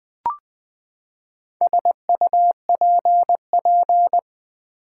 Morse code, heard as a keyed tone, about 20 wpm, 700 Hz. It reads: SUPP
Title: Build A Word - 4 Letters At A Time - 10-Letter Words - 20wpm